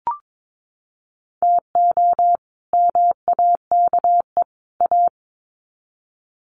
Morse code, heard as a keyed tone, about 22 wpm, 700 Hz. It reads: TO MAKE A
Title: Sets of 3 Words 22wpm